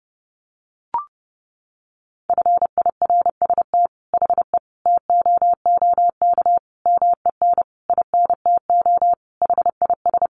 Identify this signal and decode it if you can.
Morse code, heard as a keyed tone, about 30 wpm, 700 Hz. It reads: FIRST HE TOOK MEN INTO HIS
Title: Sentences from Top 1500 Words 30wpm